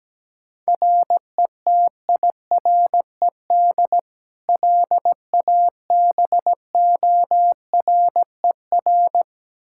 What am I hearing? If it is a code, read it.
Morse code, heard as a keyed tone, about 17 wpm, 700 Hz. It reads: RETIRED LABORER
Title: Practice QSO - Exchange 3 - More Info - 17wpm